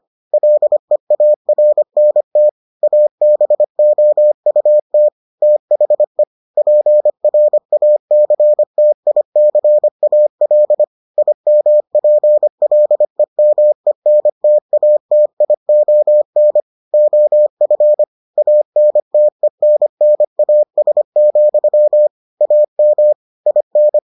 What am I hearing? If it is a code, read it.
Morse code, heard as a keyed tone, about 25 wpm, 600 Hz. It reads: LEARNT ABOUT THE PRACTICAL IMPLEMENTATION OF ANTENNAS, AM IN